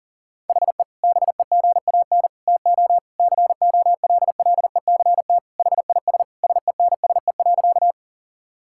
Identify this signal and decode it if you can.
Morse code, heard as a keyed tone, about 40 wpm, 700 Hz. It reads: HE BEGAN TO COLLECT HIS SENSE.
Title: Sentences from Top 1600 Words 40wpm